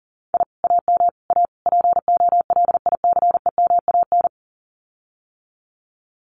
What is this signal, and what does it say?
Morse code, heard as a keyed tone, about 40 wpm, 700 Hz. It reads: I AM A POLICEMAN
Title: Common QSO Phrases Encore 40wpm